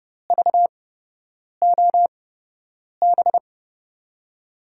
Morse code, heard as a keyed tone, about 31 wpm, spaced as for 8 wpm, 700 Hz. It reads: VOB
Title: USA Call Signs - 6 Characters - 8x Character Spacing - 30wpm